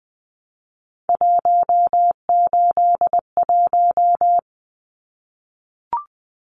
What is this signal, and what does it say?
Morse code, heard as a keyed tone, about 20 wpm, 700 Hz. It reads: 181
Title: Numbers - 3-Digit 20wpm